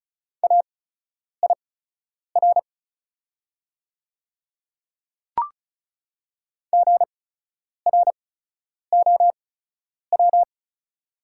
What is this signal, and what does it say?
Morse code, heard as a keyed tone, about 34 wpm, spaced as for 10 wpm, 690 Hz. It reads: AIR GROW
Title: 8x Character Spacing - Top 500 Words - 35wpm